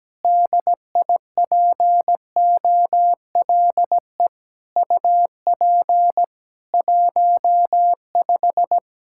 Morse code, heard as a keyed tone, about 17 wpm, 700 Hz. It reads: DIPOLE UP 15
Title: Practice QSO - Exchange 2 - Info - 17wpm